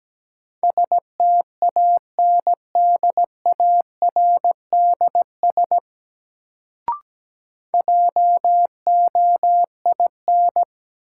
Morse code, heard as a keyed tone, about 17 wpm, 700 Hz. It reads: STANDARDS JOIN
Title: Sentences from Top 1000 Words - Review Part 6 - New Word Drill - 17wpm